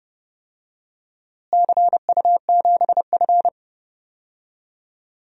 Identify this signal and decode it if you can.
Morse code, heard as a keyed tone, about 30 wpm, 700 Hz. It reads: CU7F
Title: DXCC-100 Call Signs - 4 Characters - Repeated 2x - 30wpm